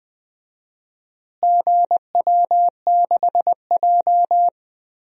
Morse code, heard as a keyed tone, about 20 wpm, 700 Hz. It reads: GW6J
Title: DXCC-100 Call Signs - 4 Characters - Repeated 3x - 20wpm